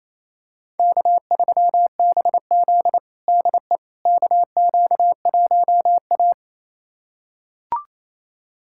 Morse code, heard as a keyed tone, about 28 wpm, 700 Hz. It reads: K3BZ DE KQ1A
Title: USA Callsign DE Callsign - 4 Characters - 28wpm